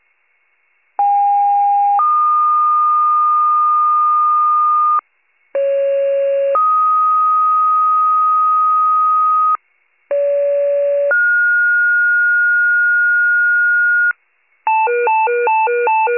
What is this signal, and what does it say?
Two-tone page, heard as a one-second tone, then a three-second tone, about 810 then 1220 Hz, about 550 then 1250 Hz, about 570 then 1430 Hz. It is heard from 1.0 to 14.1 s.